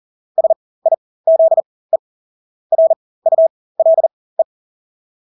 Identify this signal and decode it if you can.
Morse code, heard as a keyed tone, about 40 wpm, spaced as for 20 wpm, 650 Hz. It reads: SIZE RULE
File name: Engelske ord 40-20 wpm